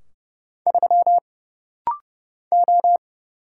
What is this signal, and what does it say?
Morse code, heard as a keyed tone, about 30 wpm, 700 Hz. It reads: 3 O
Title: Single Letter-Number Alt - Rapid-Fire 30wpm